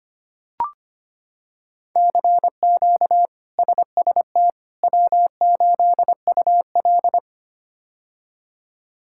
Morse code, heard as a keyed tone, about 25 wpm, 700 Hz. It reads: CQ SST W8UL
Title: K1USN SST QSO Practice - 25wpm